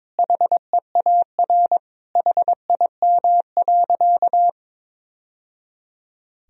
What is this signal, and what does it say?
Morse code, heard as a keyed tone, about 22 wpm, 700 Hz. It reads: HEAR HIM.
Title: Sentences from Top 1000 Words - Review Part 1 - 22wpm